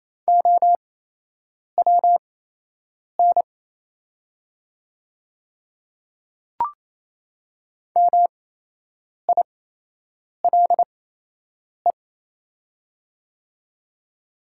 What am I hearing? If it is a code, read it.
Morse code, heard as a keyed tone, about 28 wpm, spaced as for 8 wpm, 700 Hz. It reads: OWN MILE
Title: 8x Character Spacing - Top 300 Words - 28wpm